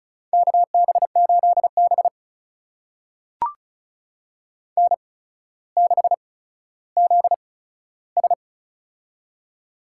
Morse code, heard as a keyed tone, about 36 wpm, spaced as for 15 wpm, 700 Hz. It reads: KB8B N6ZS
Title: USA Call Signs - 4 Characters - 8x Character Spacing - 35wpm